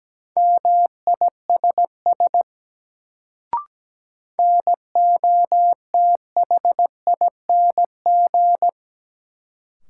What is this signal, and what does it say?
Morse code, heard as a keyed tone, about 17 wpm, 700 Hz. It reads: MISS NOTHING
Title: Sentences from Top 500 Words - New Word Drill - 17wpm